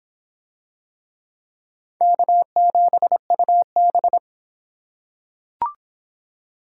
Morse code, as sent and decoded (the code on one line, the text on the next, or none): -.- --... ..- -...
K7UB